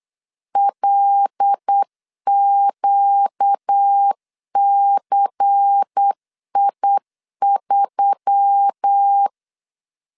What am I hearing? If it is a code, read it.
Morse code, heard as a keyed tone, about 8 wpm, 800 Hz. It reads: LQCI3
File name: LEZIONI / 6 PKX0 / 6 PK 8wpm 3003